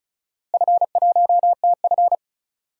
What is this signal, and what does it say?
Morse code, heard as a keyed tone, about 36 wpm, 700 Hz. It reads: F1TF
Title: DXCC-100 Call Signs - 4 Characters - Repeated 2x - 35wpm